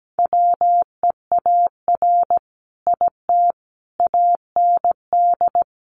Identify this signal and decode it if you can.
Morse code, heard as a keyed tone, about 17 wpm, 700 Hz. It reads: WEAR IT AND
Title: Sentences from Top 1500 Words 17wpm